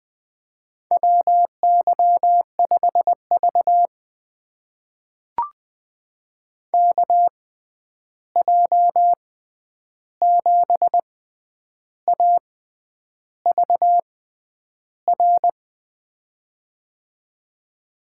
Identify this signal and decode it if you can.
Morse code, heard as a keyed tone, about 20 wpm, spaced as for 9 wpm, 700 Hz. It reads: WY5V KJ7AVR
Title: USA Call Signs - 6x Character Spacing - 20wpm